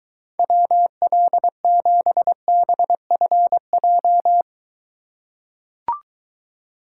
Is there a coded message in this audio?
Yes: Morse code from 0.4 to 4.4 s, heard as a keyed tone, about 23 wpm, 700 Hz. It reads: WL7BFJ